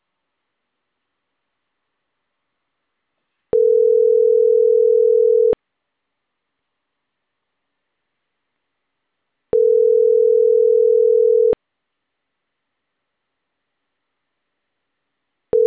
Ringback tone (North America), 440 + 480 Hz.